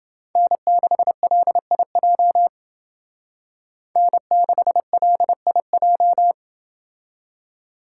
Morse code, heard as a keyed tone, about 28 wpm, 700 Hz. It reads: N6LIJ N6LIJ